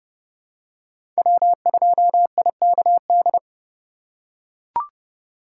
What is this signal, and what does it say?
Morse code, heard as a keyed tone, about 32 wpm, 700 Hz. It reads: W2IKD
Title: USA Call Signs - 5 Characters - 4x Character Spacing - 30wpm